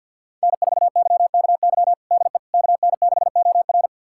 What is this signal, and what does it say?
Morse code, heard as a keyed tone, about 52 wpm, 700 Hz. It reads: N4YKY DE KN6OR
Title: USA Callsign DE Callsign 50wpm